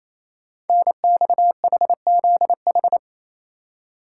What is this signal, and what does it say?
Morse code, heard as a keyed tone, about 27 wpm, 700 Hz. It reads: NXHZH
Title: Code Groups - 5 Letters - 28wpm